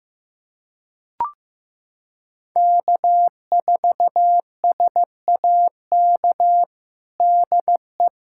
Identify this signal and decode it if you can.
Morse code, heard as a keyed tone, about 15 wpm, 700 Hz. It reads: K4SAK DE
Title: USA Callsign DE Callsign - 5 Characters - 15wpm